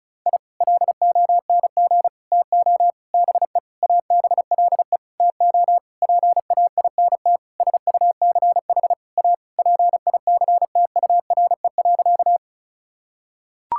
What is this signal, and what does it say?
Morse code, heard as a keyed tone, about 34 wpm, 700 Hz. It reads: I LONG TO BE ABLE TO PAINT SUCH A PICTURE.
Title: Sentences from Top 1000 Words - Review Part 6 - 35wpm